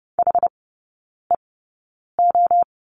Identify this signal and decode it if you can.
Morse code, heard as a keyed tone, about 30 wpm, spaced as for 9 wpm, 700 Hz. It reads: HEO
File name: 081 Lesson 14e - Number 2 - Three Characters 30wpm